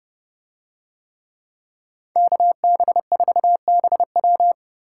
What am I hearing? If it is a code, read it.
Morse code, heard as a keyed tone, about 30 wpm, 700 Hz. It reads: KB4BW